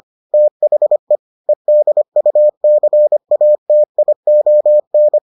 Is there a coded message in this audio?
Yes: Morse code from 0.3 to 5.2 s, heard as a keyed tone, about 25 wpm, 600 Hz. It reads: THE EDUCATION